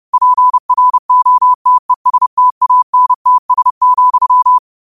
Morse code, heard as a keyed tone, about 30 wpm, 1000 Hz. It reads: PROTESTANTS,
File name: qotdfiles / 30wpm